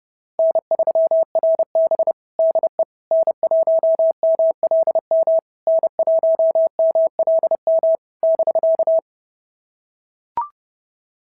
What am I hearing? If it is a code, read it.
Morse code, heard as a keyed tone, about 30 wpm, 650 Hz. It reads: N3RB DE N1MLM N1MLM <BK>
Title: SOTA QSO Practice 30wpm